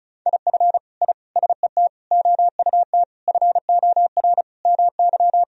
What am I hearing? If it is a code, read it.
Morse code, heard as a keyed tone, about 36 wpm, 700 Hz. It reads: IF I SET OUT FOR MY
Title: Sentences from Top 1000 Words - Review Part 2 - 35wpm